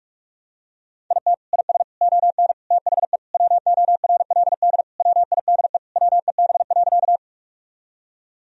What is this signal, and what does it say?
Morse code, heard as a keyed tone, about 41 wpm, 700 Hz. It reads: IT IS ON THE WORLD WIDE WEB.
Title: Sentences from Top 1000 Words - Review Part 4 - 45wpm